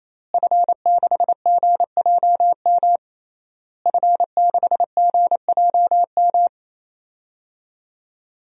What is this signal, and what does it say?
Morse code, heard as a keyed tone, about 28 wpm, 700 Hz. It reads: F6GJM F6GJM
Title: DXCC-100 Call Signs - 5 Characters - Repeated 3x - 28wpm